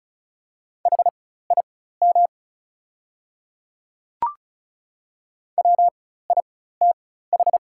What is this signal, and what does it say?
Morse code, heard as a keyed tone, about 36 wpm, spaced as for 16 wpm, 700 Hz. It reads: HIM WITH
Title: 4x Character Spacing - Top 100 Words - 35wpm